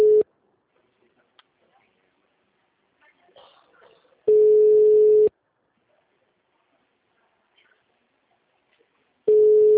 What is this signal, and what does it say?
Ringback tone (Europe, 425 Hz), with speech.